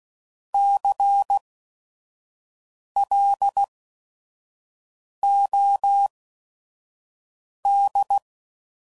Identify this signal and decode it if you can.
Morse code, heard as a keyed tone, about 16 wpm, spaced as for 5 wpm, 800 Hz. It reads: CLOD
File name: Prefixes / By First Letter / Cs